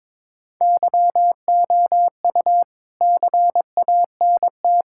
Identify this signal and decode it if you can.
Morse code, heard as a keyed tone, about 22 wpm, 700 Hz. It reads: YOU CANT
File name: Sentences with 7-Word Phrases - 22wpm